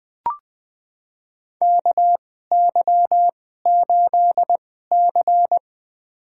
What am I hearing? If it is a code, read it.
Morse code, heard as a keyed tone, about 21 wpm, spaced as for 15 wpm, 700 Hz. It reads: KY8C
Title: USA Call Signs - 4 Characters - 2x Character Spacing - 20wpm